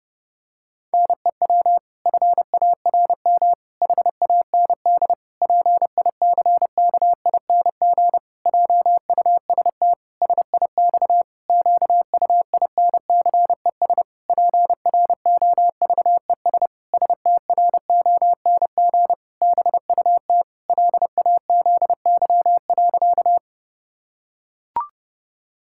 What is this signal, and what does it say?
Morse code, heard as a keyed tone, about 30 wpm, 700 Hz. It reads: NEW FARM HAND PICKING JUST SIX QUINCES PROVES STRONG BUT LAZY.
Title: Pangram Challenge v2 - 30wpm